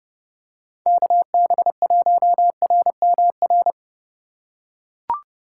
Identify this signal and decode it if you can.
Morse code, heard as a keyed tone, about 25 wpm, 700 Hz. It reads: KB1RMR